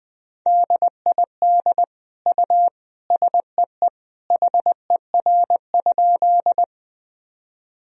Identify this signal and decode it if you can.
Morse code, heard as a keyed tone, about 20 wpm, 700 Hz. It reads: DID U SEE HER?
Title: Sentences from Top 100 Words and Abbreviations 20wpm